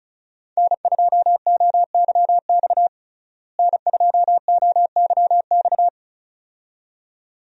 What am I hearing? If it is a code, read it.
Morse code, heard as a keyed tone, about 36 wpm, 700 Hz. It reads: N2OYX N2OYX
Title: USA Call Signs - 5 Characters - Repeated 2x - 35wpm